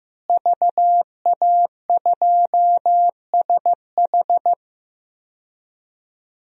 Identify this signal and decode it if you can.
Morse code, heard as a keyed tone, about 15 wpm, 700 Hz. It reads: VA2SH
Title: DXCC-100 Call Signs - 5 Characters - Repeated 3x - 15wpm